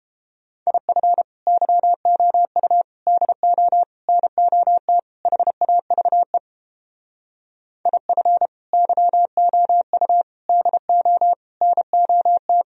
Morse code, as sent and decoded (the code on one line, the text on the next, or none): .. ..-. / -.-- --- ..- / -.. --- / -. --- - / .... .- ...- . / .. ..-. / -.-- --- ..- / -.. --- / -. --- -
IF YOU DO NOT HAVE IF YOU DO NOT